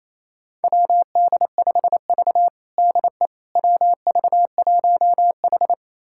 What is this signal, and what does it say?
Morse code, heard as a keyed tone, about 28 wpm, 700 Hz. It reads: WD5V DE WV1H
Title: USA Callsign DE Callsign - 4 Characters - 28wpm